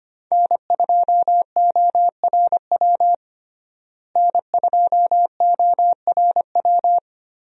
Morse code, heard as a keyed tone, about 25 wpm, 700 Hz. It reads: N2ORW N2ORW
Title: USA Call Signs - 5 Characters - Repeated 3x - 25wpm